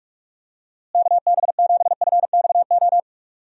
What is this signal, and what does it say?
Morse code, heard as a keyed tone, about 46 wpm, 700 Hz. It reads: KB7FXO
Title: USA Call Signs - 6 Characters - Repeated 2x - 45wpm